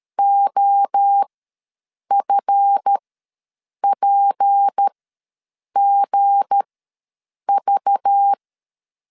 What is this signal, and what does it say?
Morse code, heard as a keyed tone, about 13 wpm, spaced as for 7 wpm, 800 Hz. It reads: OFPGV
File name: LEZIONI / 7 GF9Z / 7 GF ws 12wpm 3140